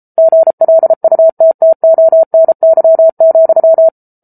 Morse code, heard as a keyed tone, about 34 wpm, 650 Hz. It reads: GLUTTONY,